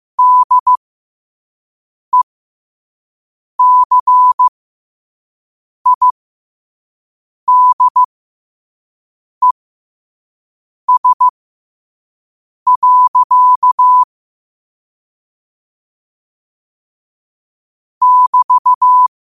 Morse code, heard as a keyed tone, about 15 wpm, spaced as for 5 wpm, 1000 Hz. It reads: DECIDES. <BT>
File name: qotdfiles / 5wpm